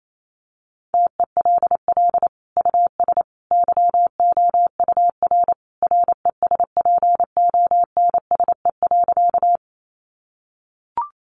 Morse code, heard as a keyed tone, about 28 wpm, 700 Hz. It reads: TELL US YOUR RESPONSE.
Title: Sentences from Top 1400 Words 28wpm